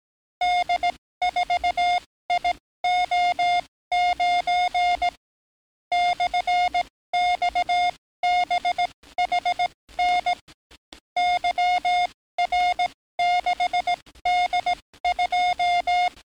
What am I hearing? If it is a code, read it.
Morse code, heard as a keyed tone, about 18 wpm, spaced as for 14 wpm, 710 Hz. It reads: D4IO9 /XBHN YR6D2
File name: training / NAVPERS 11049to11060 / NAVPERS 11057 RA RB